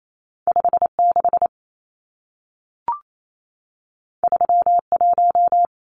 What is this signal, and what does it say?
Morse code, heard as a keyed tone, about 28 wpm, 700 Hz. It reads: 56 31
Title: Numbers - 2-Digit 28wpm